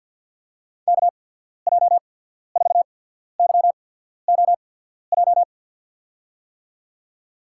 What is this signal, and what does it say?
Morse code, heard as a keyed tone, about 54 wpm, spaced as for 14 wpm, 700 Hz. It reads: KJ4YOJ
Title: USA Call Signs - 6 Characters - 8x Character Spacing - 50wpm